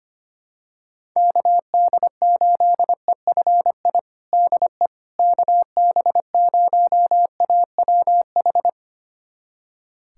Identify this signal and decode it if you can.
Morse code, heard as a keyed tone, about 25 wpm, 700 Hz. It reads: KD8EFI DE KB0AWH
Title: USA Callsign DE Callsign - 6 Characters - 25wpm